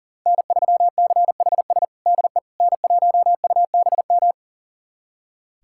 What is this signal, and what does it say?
Morse code, heard as a keyed tone, about 37 wpm, 700 Hz. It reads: N3CHS DE N1UBM
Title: USA Callsign DE Callsign - 5 Characters - 40wpm